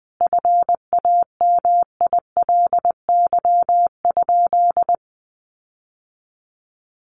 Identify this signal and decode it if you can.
Morse code, heard as a keyed tone, about 20 wpm, 700 Hz. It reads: FAMILY?
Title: Sentences from Top 1400 Words 20wpm